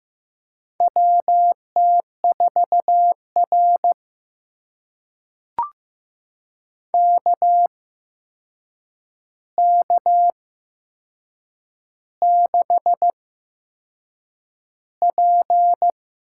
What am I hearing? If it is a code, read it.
Morse code, heard as a keyed tone, about 15 wpm, spaced as for 6 wpm, 700 Hz. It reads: WT4R KK6P
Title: USA Call Signs - 4 Characters - 8x Character Spacing - 15wpm